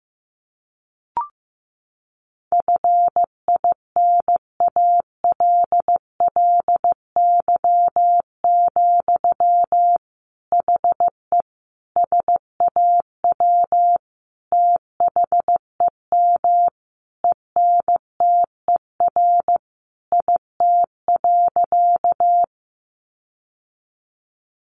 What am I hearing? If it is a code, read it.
Morse code, heard as a keyed tone, about 15 wpm, 700 Hz. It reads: FINALLY, HE SAW THEM ENTER IT.
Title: Sentences from Top 1300 Words 15wpm